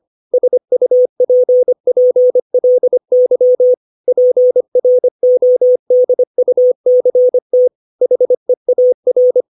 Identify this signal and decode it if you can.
Morse code, heard as a keyed tone, about 25 wpm, 500 Hz. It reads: SUPPLY PRODUCT HEAR
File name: 632 words / 632 Words 25WPM